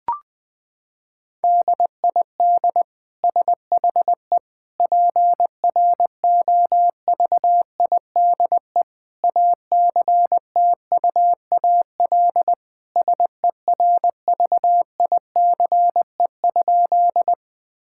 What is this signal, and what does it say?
Morse code, heard as a keyed tone, about 20 wpm, 700 Hz. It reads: DID SHE PROVIDE ACTUAL SERVICE?
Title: Sentences from Top 1000 Words - Review Part 5 - 20wpm